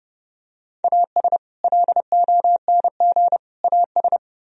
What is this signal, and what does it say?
Morse code, heard as a keyed tone, about 29 wpm, 700 Hz. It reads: AS LONG AS